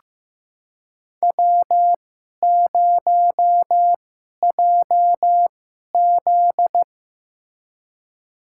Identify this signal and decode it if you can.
Morse code, heard as a keyed tone, about 15 wpm, spaced as for 11 wpm, 700 Hz. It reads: W0JZ
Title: USA Call Signs - 4 Characters - 2x Character Spacing - 15wpm